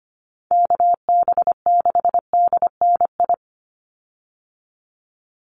Morse code, heard as a keyed tone, about 25 wpm, 700 Hz. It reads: KB6DNI
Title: USA Call Signs 25wpm